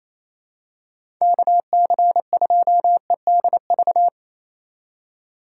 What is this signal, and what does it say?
Morse code, heard as a keyed tone, about 28 wpm, 700 Hz. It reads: KC2EDV
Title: USA Call Signs - 2x Character Spacing - 28wpm